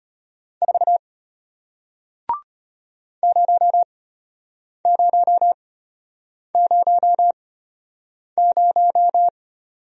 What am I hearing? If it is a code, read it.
Morse code, heard as a keyed tone, about 34 wpm, spaced as for 8 wpm, 700 Hz. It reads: 4 0000